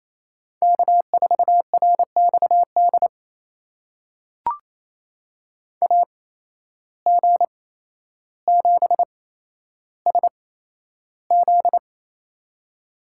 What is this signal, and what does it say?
Morse code, heard as a keyed tone, about 28 wpm, spaced as for 12 wpm, 700 Hz. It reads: K4RXD AG7SZ
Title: USA Call Signs - 5 Characters - 8x Character Spacing - 28wpm